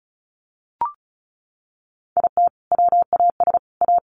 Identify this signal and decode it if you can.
Morse code, heard as a keyed tone, about 34 wpm, 700 Hz. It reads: IT WAS A
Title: Sets of 3 Words 35wpm